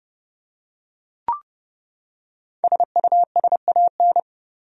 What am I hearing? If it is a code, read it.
Morse code, heard as a keyed tone, about 30 wpm, 700 Hz. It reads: SUSAN
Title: USA Common Names - 30wpm